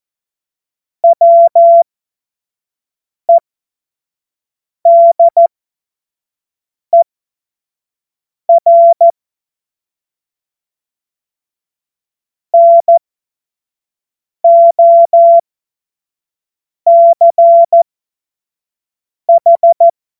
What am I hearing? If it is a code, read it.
Morse code, heard as a keyed tone, about 14 wpm, spaced as for 5 wpm, 680 Hz. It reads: WEDER NOCH
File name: Morsekurs / Witze / Witz 07. Februar 2023 25 F70